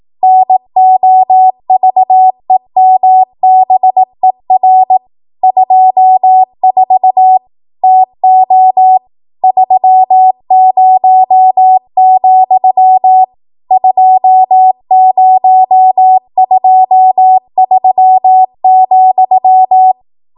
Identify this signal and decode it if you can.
Morse code, heard as a keyed tone, about 18 wpm, 750 Hz. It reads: NOVEMBER 24 TO 30, 2023,
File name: Archive / 18 WPM / 18WPM